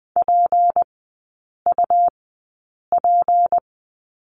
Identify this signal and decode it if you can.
Morse code, heard as a keyed tone, about 20 wpm, spaced as for 8 wpm, 700 Hz. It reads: PUP
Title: 152 Lesson 25e - Letter p - Three Characters 20wpm